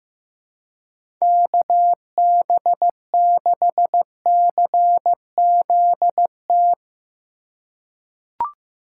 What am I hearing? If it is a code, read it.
Morse code, heard as a keyed tone, about 16 wpm, 700 Hz. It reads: KB6CZT
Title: USA Call Signs - 6 Characters - 2x Character Spacing - 15wpm